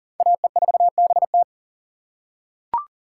Morse code, heard as a keyed tone, about 40 wpm, 700 Hz. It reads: AE4BT
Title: USA Call Signs - 5 Characters - 6x Character Spacing - 40wpm